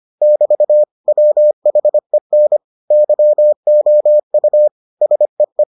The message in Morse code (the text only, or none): <BT> WHEN YOU SEE